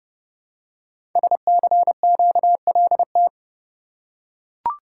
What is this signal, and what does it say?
Morse code, heard as a keyed tone, about 30 wpm, 700 Hz. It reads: SCQLT
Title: Code Groups - 5 Letters - 30wpm